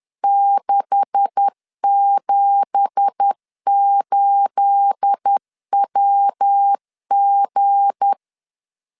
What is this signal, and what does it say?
Morse code, heard as a keyed tone, about 10 wpm, 800 Hz. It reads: 678WG
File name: LEZIONI / 9 2867 10wpm 2733